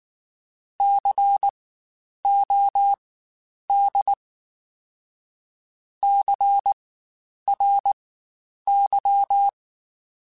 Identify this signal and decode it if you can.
Morse code, heard as a keyed tone, about 19 wpm, spaced as for 9 wpm, 800 Hz. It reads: COD CRY
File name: Spaced Alphas / Spaced Alphas mp3s / Cs sp @18